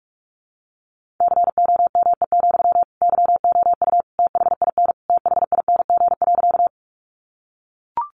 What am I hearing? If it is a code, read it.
Morse code, heard as a keyed tone, about 44 wpm, 700 Hz. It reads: COME, YOU THIN THING.
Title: Sentences from Top 1200 Words 45wpm